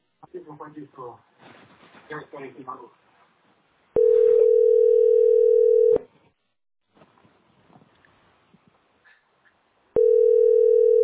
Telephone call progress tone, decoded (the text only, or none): ringback tone